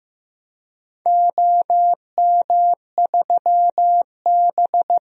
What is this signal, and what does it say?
Morse code, heard as a keyed tone, about 15 wpm, 700 Hz. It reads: OM3B
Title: DXCC-100 Call Signs - 4 Characters - Repeated 3x - 15wpm